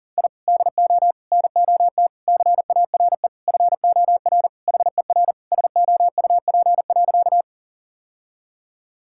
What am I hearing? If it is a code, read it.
Morse code, heard as a keyed tone, about 40 wpm, 700 Hz. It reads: I DO NOT CARE FOR HER SOUP.
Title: Sentences from Top 1000 Words - Review Part 6 - 40wpm